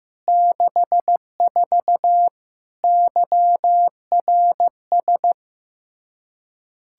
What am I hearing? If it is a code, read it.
Morse code, heard as a keyed tone, about 15 wpm, 700 Hz. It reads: 64 YRS
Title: Practice QSO - Exchange 3 - More Info - 15wpm